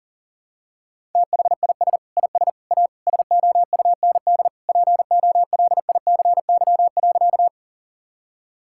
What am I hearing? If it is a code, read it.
Morse code, heard as a keyed tone, about 37 wpm, 700 Hz. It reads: THIS IS A SOUND POLICY.
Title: Sentences from Top 1000 Words - Review Part 5 - 40wpm